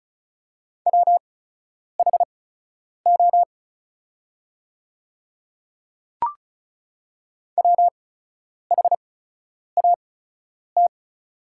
Morse code, heard as a keyed tone, about 36 wpm, spaced as for 10 wpm, 700 Hz. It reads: WHO WHAT
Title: 8x Character Spacing - Top 100 Words - 35wpm